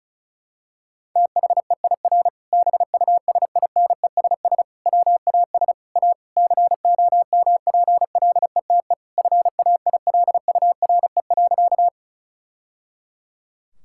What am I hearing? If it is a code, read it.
Morse code, heard as a keyed tone, about 34 wpm, 700 Hz. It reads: THEIR BUSINESS WAS A COMPLETE FAILURE.
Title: Sentences from Top 1000 Words - Review Part 6 - 35wpm